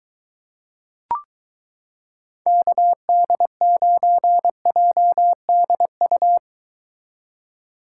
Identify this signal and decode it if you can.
Morse code, heard as a keyed tone, about 23 wpm, 700 Hz. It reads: KD9JDU